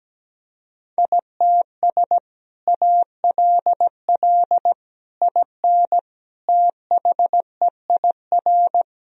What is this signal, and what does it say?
Morse code, heard as a keyed tone, about 17 wpm, 700 Hz. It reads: ITS ALL IN THEIR
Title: Sentences from Top 1000 Words - Review Part 1 - 17wpm